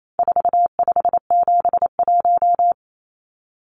Morse code, heard as a keyed tone, about 28 wpm, 700 Hz. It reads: 4571
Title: Numbers - 4-Digit 28wpm